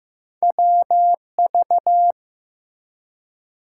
Morse code, heard as a keyed tone, about 15 wpm, 700 Hz. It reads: WV
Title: US States and Canadian Provinces 15wpm